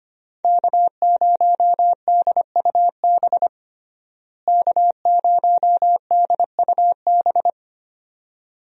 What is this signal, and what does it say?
Morse code, heard as a keyed tone, about 25 wpm, 700 Hz. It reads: K0DUB K0DUB
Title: USA Call Signs - 5 Characters - Repeated 2x - 25wpm